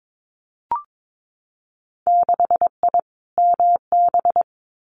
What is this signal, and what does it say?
Morse code, heard as a keyed tone, about 22 wpm, 700 Hz. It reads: 6I MB
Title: Winter Field Day Exchanges Encore 22wpm